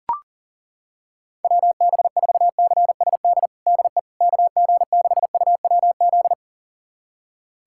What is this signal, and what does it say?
Morse code, heard as a keyed tone, about 40 wpm, 700 Hz. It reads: WB4CSD DE KG6UWZ
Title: USA Callsign DE Callsign 40wpm